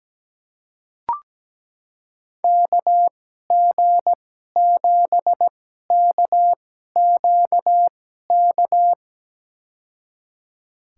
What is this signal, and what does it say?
Morse code, heard as a keyed tone, about 17 wpm, spaced as for 12 wpm, 700 Hz. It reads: KG7KQK
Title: USA Call Signs - 6 Characters - 2x Character Spacing - 17wpm